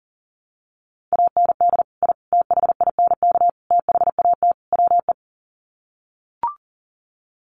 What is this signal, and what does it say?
Morse code, heard as a keyed tone, about 40 wpm, 700 Hz. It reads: AND I THINK THAT WE
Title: Sets of 5 Words Encore2 40wpm